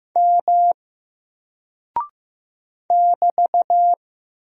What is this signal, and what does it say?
Morse code, heard as a keyed tone, about 15 wpm, 700 Hz. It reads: M <BT>
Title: Single Character (Letter:Number:Punctuation:Prosign) Alt 15wpm